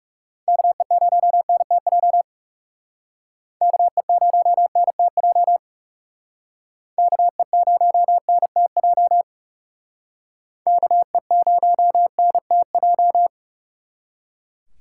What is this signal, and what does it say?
Morse code, heard as a keyed tone, about 39 wpm, 700 Hz. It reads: KE0NTJ KE0NTJ KE0NTJ KE0NTJ